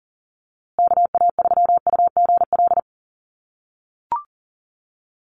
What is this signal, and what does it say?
Morse code, heard as a keyed tone, about 40 wpm, 700 Hz. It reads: KA3UGL
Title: USA Call Signs Repeated 2x 40wpm